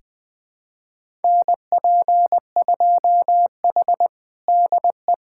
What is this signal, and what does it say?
Morse code, heard as a keyed tone, about 21 wpm, 700 Hz. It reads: NP2H DE
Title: USA Callsign DE Callsign - 4 Characters - 20wpm